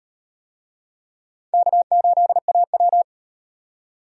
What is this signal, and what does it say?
Morse code, heard as a keyed tone, about 38 wpm, 700 Hz. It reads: K8AW